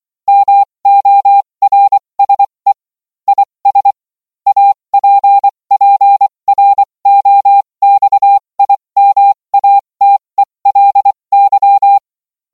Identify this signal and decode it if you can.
Morse code, heard as a keyed tone, about 24 wpm, spaced as for 20 wpm, 800 Hz. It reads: MORSE IS APPROXIMATELY